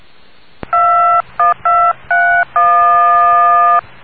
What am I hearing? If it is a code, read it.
DTMF, heard as telephone touch tones, tones of about 290 ms. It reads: 21231